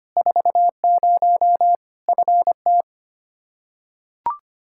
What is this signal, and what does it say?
Morse code, heard as a keyed tone, about 25 wpm, 700 Hz. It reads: 40 FT
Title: Practice QSO - Exchange 2 - Info - 25wpm